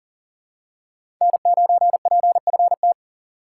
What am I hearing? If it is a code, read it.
Morse code, heard as a keyed tone, about 40 wpm, 700 Hz. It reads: N9PFT
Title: USA Call Signs - 5 Characters - 4x Character Spacing - 40wpm